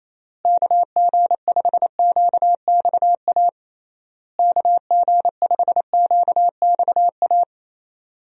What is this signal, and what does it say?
Morse code, heard as a keyed tone, about 28 wpm, 700 Hz. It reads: KG5QXA KG5QXA
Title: USA Call Signs - 6 Characters - Repeated 3x - 28wpm